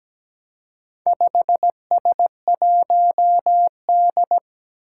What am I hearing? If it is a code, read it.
Morse code, heard as a keyed tone, about 17 wpm, 700 Hz. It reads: 5S1D
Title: Four Challenging Letter-Number Combinations 17wpm